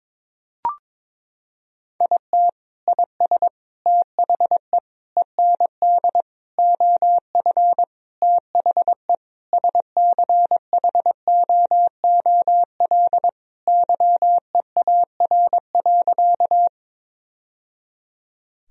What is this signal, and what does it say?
Morse code, heard as a keyed tone, about 22 wpm, 700 Hz. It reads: IT IS THE END OF THE SCHOOL YEAR.
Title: Sentences from Top 1000 Words - Review Part 1 - 22wpm